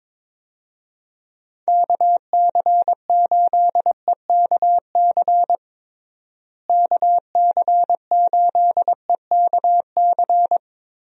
Morse code, heard as a keyed tone, about 21 wpm, 700 Hz. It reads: KC8EKC KC8EKC
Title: USA Call Signs - 6 Characters - Repeated 2x - 22wpm